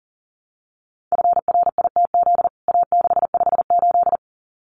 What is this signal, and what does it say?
Morse code, heard as a keyed tone, about 37 wpm, 700 Hz. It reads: FRITZ A658
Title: CWT Exchange 40wpm